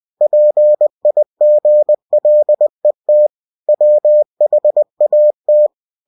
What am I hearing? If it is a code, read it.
Morse code, heard as a keyed tone, about 20 wpm, 600 Hz. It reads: PIGLET WHAT
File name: cw-books / winnie-the-pooh / winnie-20wpm-page-0020